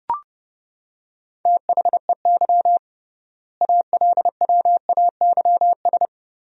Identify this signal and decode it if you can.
Morse code, heard as a keyed tone, about 32 wpm, 700 Hz. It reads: THEY ALWAYS
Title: 3x Word-Spacing - Sentences from Top 300 Words - 30wpm